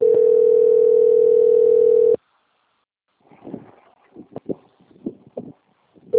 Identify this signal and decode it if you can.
Ringback tone (North America), 440 + 480 Hz.
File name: out- - - - - .53255